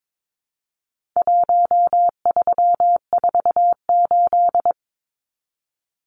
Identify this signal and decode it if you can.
Morse code, heard as a keyed tone, about 22 wpm, 700 Hz. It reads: 1348
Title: Numbers - Contest Style - 4-Digit 22wpm